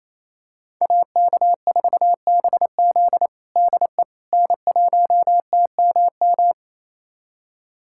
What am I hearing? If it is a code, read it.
Morse code, heard as a keyed tone, about 28 wpm, 700 Hz. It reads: AK4BZ DE N1TMM
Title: USA Callsign DE Callsign - 5 Characters - 28wpm